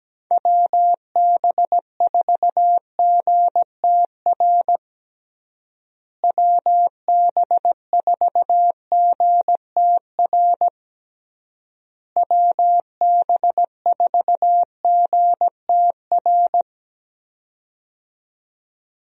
Morse code, heard as a keyed tone, about 17 wpm, 700 Hz. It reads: WB4GTR WB4GTR WB4GTR
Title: USA Call Signs - 6 Characters - Repeated 3x - 17wpm